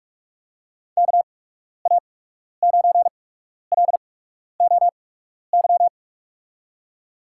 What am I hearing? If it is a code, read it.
Morse code, heard as a keyed tone, about 46 wpm, spaced as for 12 wpm, 700 Hz. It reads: KA9LOY
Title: USA Call Signs - 6 Characters - 8x Character Spacing - 45wpm